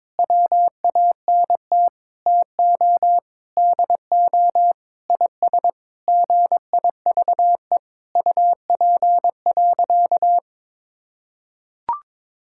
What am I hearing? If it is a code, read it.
Morse code, heard as a keyed tone, about 22 wpm, 700 Hz. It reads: WANT TO DO IS GIVE UP.
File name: practice / 22wpm / Sentences with 8-Word Phrases - 22wpm